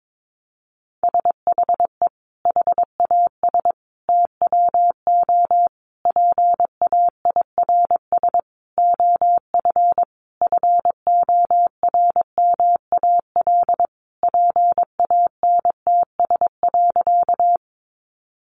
Morse code, heard as a keyed tone, about 22 wpm, 700 Hz. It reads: SHE HAS TWO PAIRS OF FORMAL PANTS.
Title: Additional Easy Phrases 22wpm